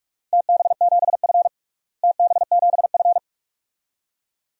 Morse code, heard as a keyed tone, about 41 wpm, 700 Hz. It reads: TB7F TB7F
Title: DXCC-100 Call Signs - 4 Characters - Repeated 2x - 45wpm